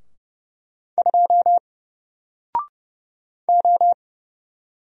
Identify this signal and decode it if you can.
Morse code, heard as a keyed tone, about 30 wpm, 700 Hz. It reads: 2 O
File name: Single-Letter-Number-30wpm